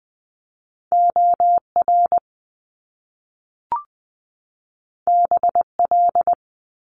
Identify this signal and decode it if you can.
Morse code, heard as a keyed tone, about 20 wpm, 700 Hz. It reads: OR BL
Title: Top Two Letter Combinations in Words 20wpm